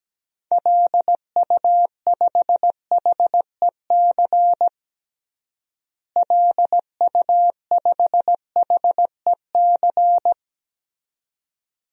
Morse code, heard as a keyed tone, about 17 wpm, 700 Hz. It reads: LU5HEC LU5HEC
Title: DXCC-100 Call Signs - 6 Characters - Repeated 2x - 17wpm